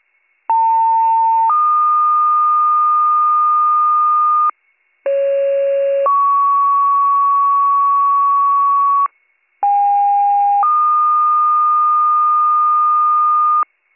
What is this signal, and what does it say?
Two-tone page, heard as a one-second tone, then a three-second tone, about 900 then 1220 Hz, about 550 then 1120 Hz, about 810 then 1220 Hz. It is heard from 0.5 to 13.6 s.